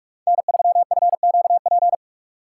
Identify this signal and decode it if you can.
Morse code, heard as a keyed tone, about 41 wpm, 700 Hz. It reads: N3FQP
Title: USA Call Signs - 5 Characters - 4x Character Spacing - 45wpm